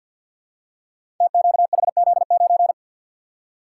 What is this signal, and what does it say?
Morse code, heard as a keyed tone, about 51 wpm, 700 Hz. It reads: TQHZ9
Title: Code Groups - 5 Letter:Numbers - 50wpm